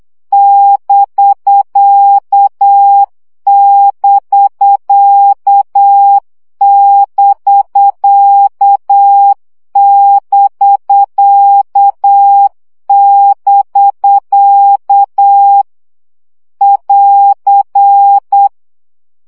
Morse code, heard as a keyed tone, about 8 wpm, 800 Hz. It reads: <BK><BK><BK><BK><BK> <AR>
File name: CW gyakorlas bk 7 baud